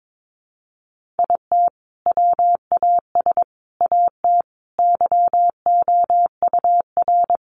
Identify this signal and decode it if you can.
Morse code, heard as a keyed tone, about 22 wpm, 700 Hz. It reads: IT WAS AT YOUR
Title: Sentences from Top 1500 Words 22wpm